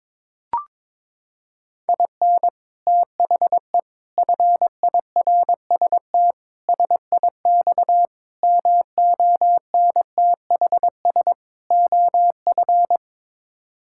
Morse code, heard as a keyed tone, about 22 wpm, 700 Hz. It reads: IN THE FIRST SIX MONTHS OF
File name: practice / 22wpm / Sets of 6 Words - 22wpm